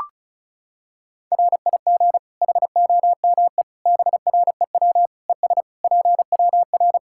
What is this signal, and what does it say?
Morse code, heard as a keyed tone, about 34 wpm, 700 Hz. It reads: RIG HOME BREW ES PWR
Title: Practice QSO - Exchange 2 - Info - 35wpm